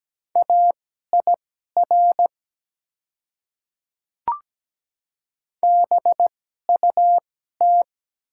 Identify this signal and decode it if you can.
Morse code, heard as a keyed tone, about 17 wpm, spaced as for 12 wpm, 700 Hz. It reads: AIR BUT
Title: 2x Character Spacing - Top 200 Words - 17wpm